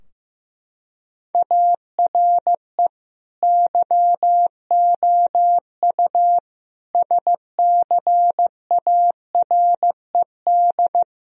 Morse code, heard as a keyed tone, about 16 wpm, 700 Hz. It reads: ARE YOU SCARED
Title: Sentences from Top 1000 Words - Review Part 4 - 15wpm